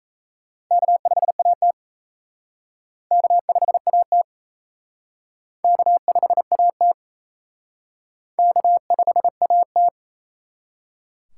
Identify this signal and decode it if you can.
Morse code, heard as a keyed tone, about 34 wpm, 700 Hz. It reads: K5AT K5AT K5AT K5AT
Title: Speed Racing - USA 1x2 Call Signs 28wpm